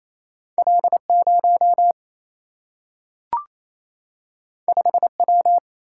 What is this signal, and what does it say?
Morse code, heard as a keyed tone, about 28 wpm, 700 Hz. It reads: L0 5W
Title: Single-Letter Single-Number 28wpm